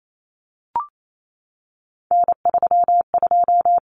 Morse code, heard as a keyed tone, about 28 wpm, 700 Hz. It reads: N32
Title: Common Cut Numbers - 3-Digit 28wpm